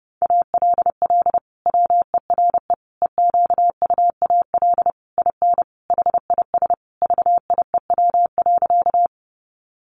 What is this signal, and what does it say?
Morse code, heard as a keyed tone, about 30 wpm, 700 Hz. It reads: ALL WERE EQUAL IN HIS VIEW.
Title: Sentences from Top 1400 Words 30wpm